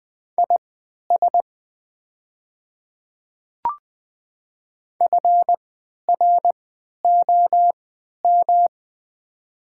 Morse code, heard as a keyed tone, about 20 wpm, spaced as for 11 wpm, 700 Hz. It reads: IS FROM
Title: 3x Character Spacing - Top 100 Words - 20wpm